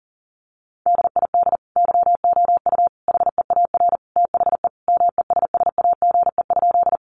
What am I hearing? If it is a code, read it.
Morse code, heard as a keyed tone, about 40 wpm, 700 Hz. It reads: DID YOU HEAR THE MESSAGE?
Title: Sentences from Top 1100 Words 40wpm